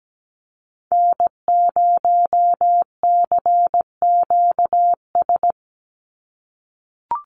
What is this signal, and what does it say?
Morse code, heard as a keyed tone, about 17 wpm, 700 Hz. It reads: N0CQS
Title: USA Call Signs 17wpm